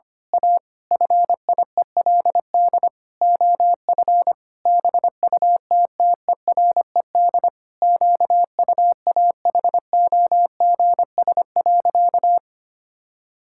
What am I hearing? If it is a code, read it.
Morse code, heard as a keyed tone, about 25 wpm, 700 Hz. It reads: A FIELD OF BUTTERED QUAHOGS.
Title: Pangram Challenge v2 - 25wpm